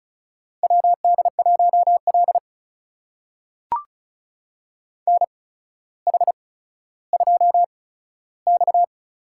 Morse code, heard as a keyed tone, about 36 wpm, spaced as for 15 wpm, 700 Hz. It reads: WD1L NH2X
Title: USA Call Signs - 4 Characters - 8x Character Spacing - 35wpm